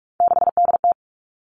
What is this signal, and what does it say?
Morse code, heard as a keyed tone, about 46 wpm, 700 Hz. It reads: 6DT
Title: Three Challenging Letter-Number Combinations 45wpm